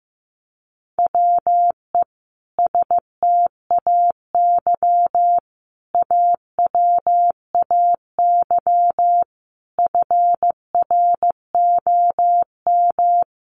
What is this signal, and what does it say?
Morse code, heard as a keyed tone, about 15 wpm, 700 Hz. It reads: WE STAY AWAY FROM
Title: A Sound Of Thunder 15wpm